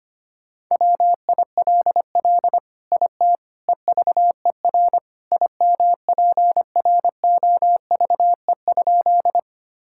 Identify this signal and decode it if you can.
Morse code, heard as a keyed tone, about 25 wpm, 700 Hz. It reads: WILL IT EVER IMPROVE?
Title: Sentences from Top 1000 Words - Review Part 5 - 25wpm